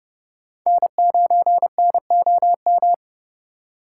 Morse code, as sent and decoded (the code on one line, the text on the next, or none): -. ----. -. --- --
N9NOM